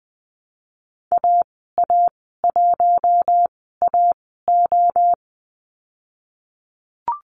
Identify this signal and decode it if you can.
Morse code, heard as a keyed tone, about 20 wpm, spaced as for 15 wpm, 700 Hz. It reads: AA1AO
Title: 055 Lesson 10g - Letter r - Call Signs 20wpm